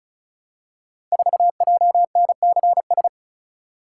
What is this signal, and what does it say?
Morse code, heard as a keyed tone, about 36 wpm, 700 Hz. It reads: 4JNCS
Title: Code Groups - 5 Letter:Numbers - 35wpm